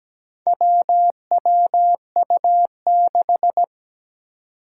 Morse code, heard as a keyed tone, about 17 wpm, 700 Hz. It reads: WWU6
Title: Four Challenging Letter-Number Combinations 17wpm